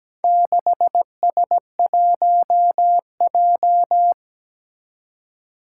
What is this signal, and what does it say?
Morse code, heard as a keyed tone, about 17 wpm, 700 Hz. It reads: 6S1J
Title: Four Challenging Letter-Number Combinations 17wpm